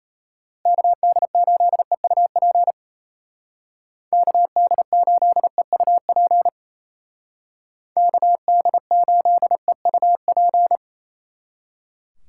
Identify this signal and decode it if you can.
Morse code, heard as a keyed tone, about 32 wpm, 700 Hz. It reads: KD8EUP KD8EUP KD8EUP
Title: Speed Racing - USA 2x3 Call Signs 28wpm